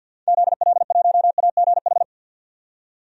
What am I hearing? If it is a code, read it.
Morse code, heard as a keyed tone, about 50 wpm, 700 Hz. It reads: ZL1AGH
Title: DXCC-100 Call Signs - 6 Characters - 50wpm